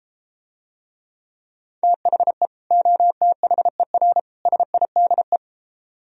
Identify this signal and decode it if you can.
Morse code, heard as a keyed tone, about 33 wpm, 700 Hz. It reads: THE OTHER SIDE